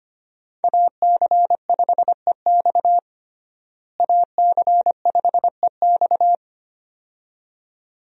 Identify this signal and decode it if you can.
Morse code, heard as a keyed tone, about 25 wpm, 700 Hz. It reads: AC5EX AC5EX
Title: USA Call Signs - 5 Characters - Repeated 2x - 25wpm